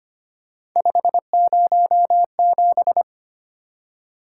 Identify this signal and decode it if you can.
Morse code, heard as a keyed tone, about 25 wpm, 700 Hz. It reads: 507
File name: Three Letter-Number Combinations - 25wpm